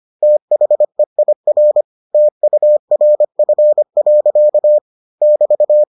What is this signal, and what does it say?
Morse code, heard as a keyed tone, about 25 wpm, 600 Hz. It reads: THEIR TURF. <BT>